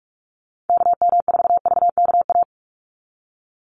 Morse code, heard as a keyed tone, about 44 wpm, 700 Hz. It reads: KM4VKA
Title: USA Call Signs 45wpm